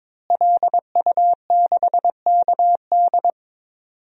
Morse code, heard as a keyed tone, about 21 wpm, 700 Hz. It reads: LU6KD
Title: DXCC-100 Call Signs - 5 Characters - Repeated 2x - 22wpm